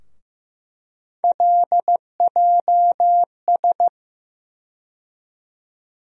Morse code, heard as a keyed tone, about 15 wpm, 700 Hz. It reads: LJS
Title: USA Call Sign Postfixes - 3 Letters 15wpm